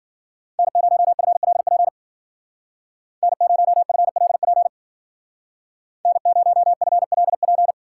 Morse code, heard as a keyed tone, about 49 wpm, 700 Hz. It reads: N0FLP N0FLP N0FLP
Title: Speed Racing - USA 1x3 Call Signs 40wpm